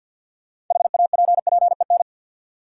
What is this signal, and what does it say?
Morse code, heard as a keyed tone, about 49 wpm, 700 Hz. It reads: HAPPEN
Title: Sentences from Top 400 Words - New Word Drill - 50wpm